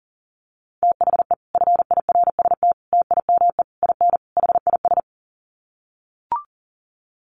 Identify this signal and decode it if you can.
Morse code, heard as a keyed tone, about 40 wpm, 700 Hz. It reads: THE FIRST TIME IN HIS
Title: Sets of 5 Words Encore2 40wpm